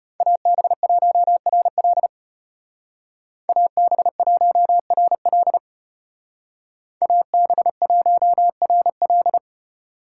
Morse code, heard as a keyed tone, about 33 wpm, 700 Hz. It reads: AB1RL AB1RL AB1RL